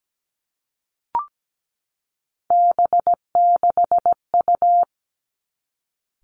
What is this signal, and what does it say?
Morse code, heard as a keyed tone, about 17 wpm, 700 Hz. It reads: B6U
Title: Three Challenging Letter-Number Combinations 17wpm